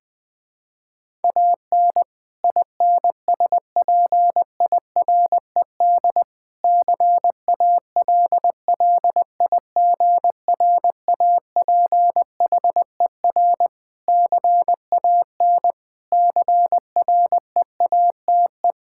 Morse code, heard as a keyed tone, about 20 wpm, 700 Hz. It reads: AN INSPIRED CALLIGRAPHER CAN CREATE
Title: Pangram Challenge v2 - 20wpm